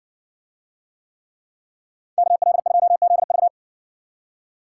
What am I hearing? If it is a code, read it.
Morse code, heard as a keyed tone, about 55 wpm, 700 Hz. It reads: KL2ZV